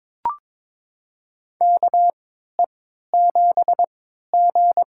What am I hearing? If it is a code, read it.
Morse code, heard as a keyed tone, about 22 wpm, spaced as for 13 wpm, 700 Hz. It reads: KE7G
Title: USA Call Signs - 4 Characters - 3x Character Spacing - 22wpm